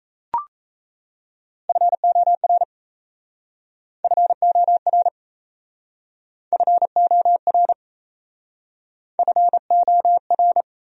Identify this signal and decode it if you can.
Morse code, heard as a keyed tone, about 38 wpm, 700 Hz. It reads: FOR FOR FOR FOR